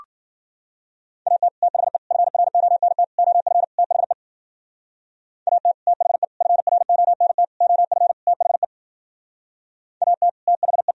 Morse code, heard as a keyed tone, about 48 wpm, 700 Hz. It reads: AT THE FRONT OF THE AT THE FRONT OF THE AT THE
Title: Speed-Racing - Sets of 5 Words 40wpm